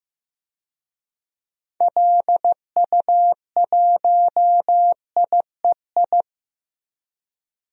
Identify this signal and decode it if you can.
Morse code, heard as a keyed tone, about 15 wpm, 700 Hz. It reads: LU1IEI
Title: DXCC-100 Call Signs - 6 Characters - Repeated 2x - 15wpm